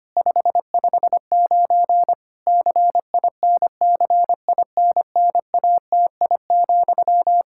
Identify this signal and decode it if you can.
Morse code, heard as a keyed tone, about 25 wpm, 700 Hz. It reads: 559 CINCINNATI,
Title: Practice QSO - Exchange 1 - Beginning - 25wpm